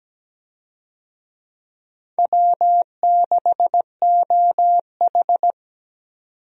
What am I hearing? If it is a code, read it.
Morse code, heard as a keyed tone, about 17 wpm, 700 Hz. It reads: W6OH
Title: USA Call Signs - 4 Characters - 17wpm